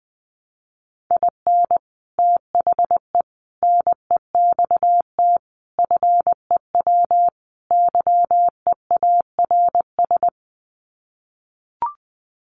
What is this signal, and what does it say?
Morse code, heard as a keyed tone, about 20 wpm, 700 Hz. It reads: IN THE NEXT FEW YEARS
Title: Sets of 5 Words Encore 20wpm